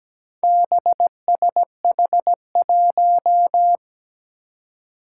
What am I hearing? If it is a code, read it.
Morse code, heard as a keyed tone, about 17 wpm, 700 Hz. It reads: BSH1
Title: Four Challenging Letter-Number Combinations 17wpm